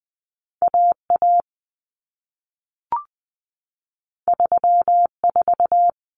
Morse code, heard as a keyed tone, about 20 wpm, 700 Hz. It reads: AA 34
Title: Common Cut Numbers - 2-Digit 20wpm